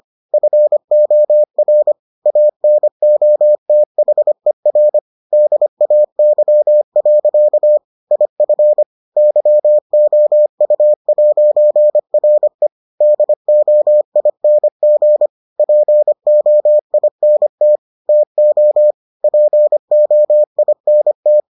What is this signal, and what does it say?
Morse code, heard as a keyed tone, about 25 wpm, 600 Hz. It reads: FOR ANOTHER DAY. IF YOU'RE DOING POINT TO POINT